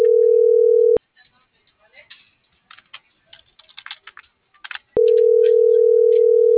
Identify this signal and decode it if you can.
Ringback tone (North America), 440 + 480 Hz.